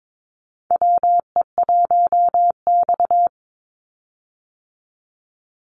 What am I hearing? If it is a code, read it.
Morse code, heard as a keyed tone, about 22 wpm, 700 Hz. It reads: WE1X
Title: USA Call Signs 22wpm